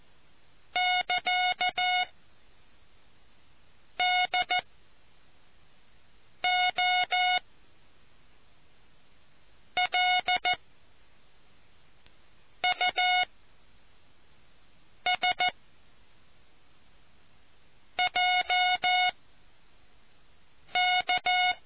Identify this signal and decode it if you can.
Morse code, heard as a keyed tone, about 14 wpm, 730 Hz. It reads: <KA> D O L U S J K